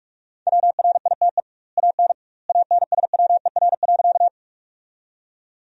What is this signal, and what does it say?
Morse code, heard as a keyed tone, about 46 wpm, 700 Hz. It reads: WRITE AN ANSWER.
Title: 2x Word-Spacing - Sentences from Top 300 Words - 45wpm